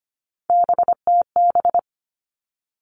Morse code, heard as a keyed tone, about 25 wpm, 700 Hz. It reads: BTB
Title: All Cut Numbers - 3-Digit 25wpm